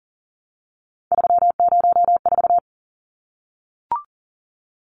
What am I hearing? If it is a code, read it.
Morse code, heard as a keyed tone, about 40 wpm, 700 Hz. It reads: 304
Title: Numbers - 3-Digit 40wpm